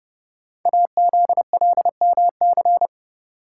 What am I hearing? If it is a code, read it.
Morse code, heard as a keyed tone, about 31 wpm, 700 Hz. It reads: AZLMC
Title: Code Groups - 5 Letters - 30wpm